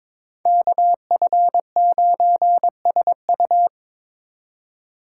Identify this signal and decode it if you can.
Morse code, heard as a keyed tone, about 25 wpm, 700 Hz. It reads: KF9SU
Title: Speed Racing - USA 2x2 Call Signs 22wpm